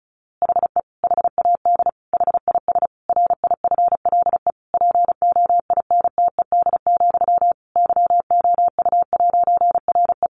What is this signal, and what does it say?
Morse code, heard as a keyed tone, about 34 wpm, 700 Hz. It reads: HE HAD HIS RIFLE POINTED, YOU'RE
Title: A Sound Of Thunder 35wpm